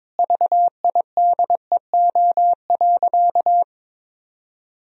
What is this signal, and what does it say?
Morse code, heard as a keyed tone, about 22 wpm, 700 Hz. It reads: VIDEO.
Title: Sentences from Top 1000 Words - Review Part 4 - 22wpm